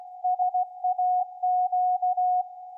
Morse code, heard as a keyed tone, about 16 wpm, 730 Hz. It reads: SAQ